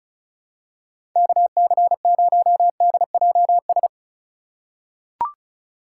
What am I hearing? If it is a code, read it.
Morse code, heard as a keyed tone, about 36 wpm, 700 Hz. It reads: KC0DJS
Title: USA Call Signs - 8x Character Spacing - 35wpm